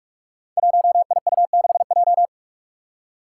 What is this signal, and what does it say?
Morse code, heard as a keyed tone, about 44 wpm, 700 Hz. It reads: 1IU6J
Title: Five Challenging Letter-Number Combinations 45wpm